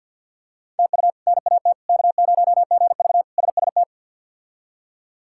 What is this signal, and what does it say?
Morse code, heard as a keyed tone, about 49 wpm, 700 Hz. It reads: TU NAT K0GV SST
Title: K1USN SST QSO Practice - 50wpm